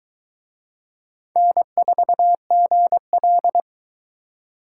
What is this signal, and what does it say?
Morse code, heard as a keyed tone, about 21 wpm, 700 Hz. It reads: N4GL